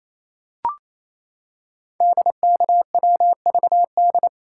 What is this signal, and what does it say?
Morse code, heard as a keyed tone, about 29 wpm, 700 Hz. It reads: DKWVD
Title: Code Groups - 5 Letters - 28wpm